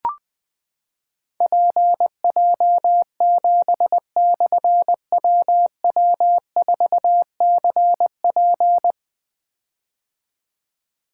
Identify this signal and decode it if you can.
Morse code, heard as a keyed tone, about 20 wpm, 700 Hz. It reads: PJ7/WW4CP
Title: Difficult Call Signs 20wpm